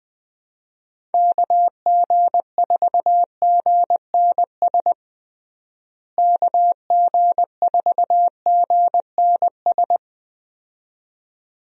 Morse code, heard as a keyed tone, about 20 wpm, 700 Hz. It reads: KG4GNS KG4GNS
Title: USA Call Signs - 6 Characters - Repeated 2x - 20wpm